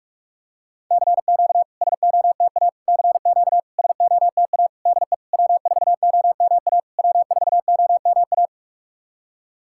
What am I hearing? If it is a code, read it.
Morse code, heard as a keyed tone, about 46 wpm, 690 Hz. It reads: CQ SOTA CQ SOTA DE W4OMA W4OMA
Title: SOTA QSO Practice 45wpm